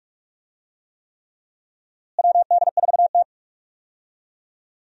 Morse code, heard as a keyed tone, about 45 wpm, 700 Hz. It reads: WD4T
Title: USA Call Signs - 4 Characters - Repeated 3x - 45wpm